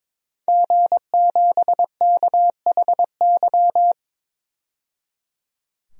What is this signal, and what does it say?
Morse code, heard as a keyed tone, about 22 wpm, 700 Hz. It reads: G7KHY
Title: DXCC-100 Call Signs - 5 Characters - 22wpm